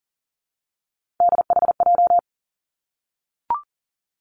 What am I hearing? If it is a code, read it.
Morse code, heard as a keyed tone, about 41 wpm, 700 Hz. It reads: DHJ
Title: Three Challenging Letter-Number Combinations 40wpm